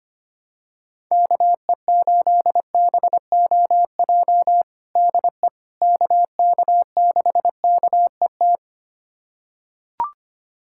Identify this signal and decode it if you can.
Morse code, heard as a keyed tone, about 25 wpm, 700 Hz. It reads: KE8BOJ DE KK6KET
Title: USA Callsign DE Callsign - 6 Characters - 25wpm